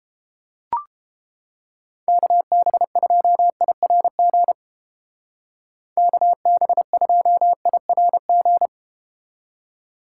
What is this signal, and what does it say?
Morse code, heard as a keyed tone, about 31 wpm, 700 Hz. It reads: KB2IRG KB2IRG